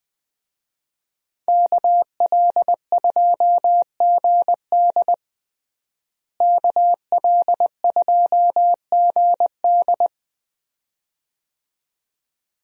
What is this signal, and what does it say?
Morse code, heard as a keyed tone, about 20 wpm, 700 Hz. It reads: KL2GD KL2GD
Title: USA Call Signs - 5 Characters - Repeated 2x - 20wpm